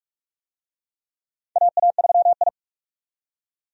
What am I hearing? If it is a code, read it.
Morse code, heard as a keyed tone, about 46 wpm, 700 Hz. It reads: AA3I
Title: USA Call Signs - 4 Characters - 45wpm